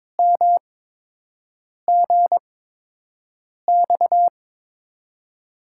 Morse code, heard as a keyed tone, about 21 wpm, spaced as for 6 wpm, 700 Hz. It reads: MGX
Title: USA Call Signs - 6 Characters - 8x Character Spacing - 22wpm